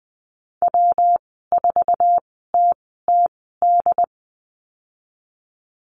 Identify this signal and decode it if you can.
Morse code, heard as a keyed tone, about 20 wpm, spaced as for 14 wpm, 700 Hz. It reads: W4TTD
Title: 161 Lesson 26g - Letter g - Call Signs 20wpm